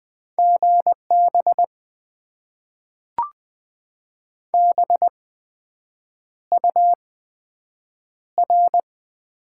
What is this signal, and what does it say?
Morse code, heard as a keyed tone, about 20 wpm, spaced as for 7 wpm, 700 Hz. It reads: GB BUR
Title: Top 90 QSO Elements - 8x Character Spacing - 20wpm